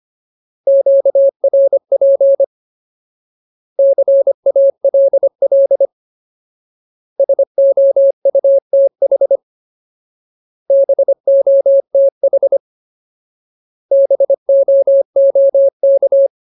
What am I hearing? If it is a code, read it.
Morse code, heard as a keyed tone, about 25 wpm, 550 Hz. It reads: QRP CALL SOUTH BOTH BOOK
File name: Ultimate Word List / 25WPM Ultimate